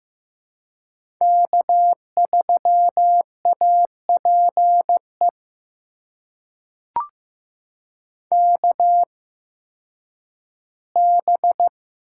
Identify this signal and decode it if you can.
Morse code, heard as a keyed tone, about 15 wpm, spaced as for 10 wpm, 700 Hz. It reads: K3APE KB
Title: USA Call Signs - 8x Character Spacing - 15wpm